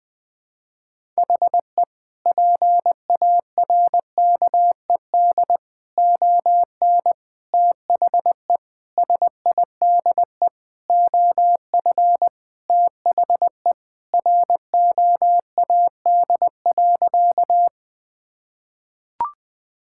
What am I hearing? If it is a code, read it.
Morse code, heard as a keyed tone, about 19 wpm, 700 Hz. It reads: HE PARKED ON THE SIDE OF THE ROAD.
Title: Sentences with 5-Word Phrases - 20wpm